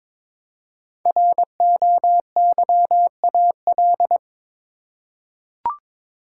Morse code, heard as a keyed tone, about 22 wpm, 700 Hz. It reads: ROYAL
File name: 5 Letter Words - First 3 Letters Given - 22wpm